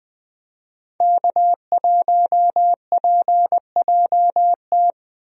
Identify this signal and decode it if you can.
Morse code, heard as a keyed tone, about 20 wpm, 700 Hz. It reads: K1PJT
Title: USA Call Signs - 5 Characters - Repeated 3x - 20wpm